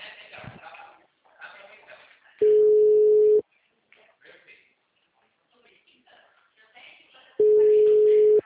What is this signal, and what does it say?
Ringback tone (Europe, 425 Hz), with speech.
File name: out- -2021- - - .19